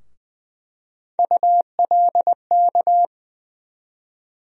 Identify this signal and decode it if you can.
Morse code, heard as a keyed tone, about 20 wpm, 700 Hz. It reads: ULK
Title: USA Call Sign Postfixes - 3 Letters 20wpm